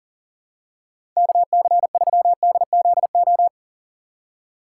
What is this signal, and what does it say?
Morse code, heard as a keyed tone, about 40 wpm, 700 Hz. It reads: KC3DZO
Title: USA Call Signs - 3x Character Spacing - 40wpm